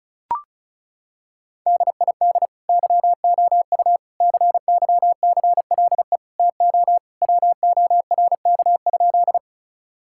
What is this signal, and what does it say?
Morse code, heard as a keyed tone, about 36 wpm, 700 Hz. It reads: DID YOU CYCLE TO WORK?
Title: Sentences from Top 1000 Words - Review Part 6 - 35wpm